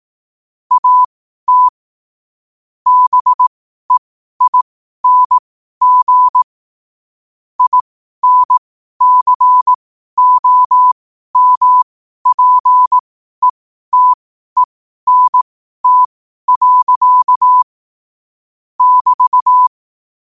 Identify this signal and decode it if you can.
Morse code, heard as a keyed tone, about 18 wpm, spaced as for 12 wpm, 1000 Hz. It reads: AT BEING INCOMPETENT. <BT>
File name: qotdfiles / 12wpm